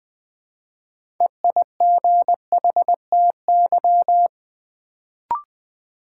Morse code, heard as a keyed tone, about 20 wpm, 700 Hz. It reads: EIGHTY
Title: Build A Word - 3 Letters At A Time - 6-Letter Words - 20wpm